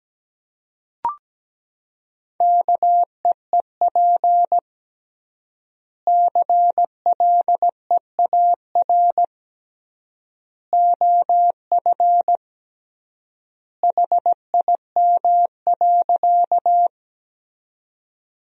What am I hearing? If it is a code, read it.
Morse code, heard as a keyed tone, about 17 wpm, 700 Hz. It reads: KEEP CLEAR OF HIM.
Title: 3x Word-Spacing - Sentences from Top 500 Words - 17wpm